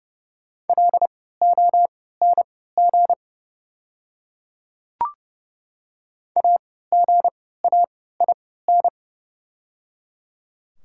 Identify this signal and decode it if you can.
Morse code, heard as a keyed tone, about 30 wpm, spaced as for 17 wpm, 700 Hz. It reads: LONG AGAIN
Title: 3x Character Spacing - Top 200 Words - 30wpm